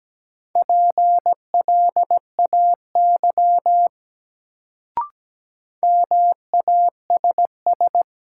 Morse code, heard as a keyed tone, about 17 wpm, 700 Hz. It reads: PLAY MASS
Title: Sets of 5 Words Encore - New Word Drill - 17wpm